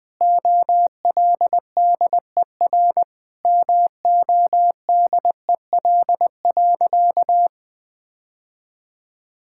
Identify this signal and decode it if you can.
Morse code, heard as a keyed tone, about 20 wpm, 700 Hz. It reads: OLDER MODEL.
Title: Sentences with 7-Word Phrases - 20wpm